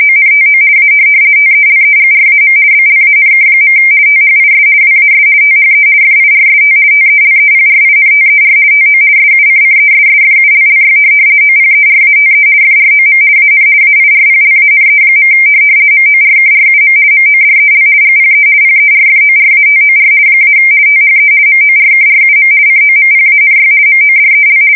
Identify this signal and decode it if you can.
RTTY, about 50 baud. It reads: ONTO A TELEGRAM BLANK. THIS WAS THE FIRST TELEPRINTER TO SUCCESSFULLY OPERATE FROM AN AIRPLANE. MORKRUM MODEL GPE PERFORATOR - THE MORKRUM COMPANY MODEL
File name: fsk170-50B/teletype-corp